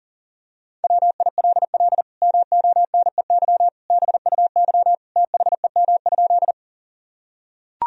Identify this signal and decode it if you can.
Morse code, heard as a keyed tone, about 40 wpm, 700 Hz. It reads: WILL MONEY BUY THEM?
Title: Sentences from Top 1000 Words - Review Part 4 - 40wpm